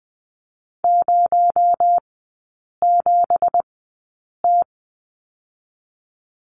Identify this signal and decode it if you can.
Morse code, heard as a keyed tone, about 20 wpm, spaced as for 8 wpm, 700 Hz. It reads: 07T
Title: 218 Lesson 35e - Number 0 - Three Characters 20wpm